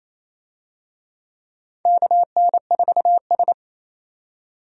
Morse code, heard as a keyed tone, about 28 wpm, 700 Hz. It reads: KN4S
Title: USA Call Signs - 4 Characters - 28wpm